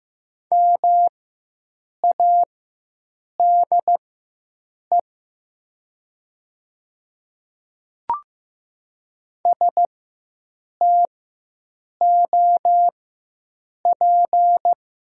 Morse code, heard as a keyed tone, about 15 wpm, spaced as for 7 wpm, 700 Hz. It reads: MADE STOP
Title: 4x Character Spacing - Top 300 Words - 15wpm